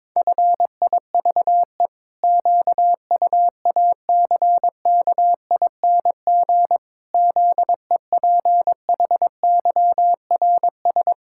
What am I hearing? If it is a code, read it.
Morse code, heard as a keyed tone, about 22 wpm, 700 Hz. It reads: FIVE QUACKING ZEPHYRS
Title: Pangram Challenge v2 - 22wpm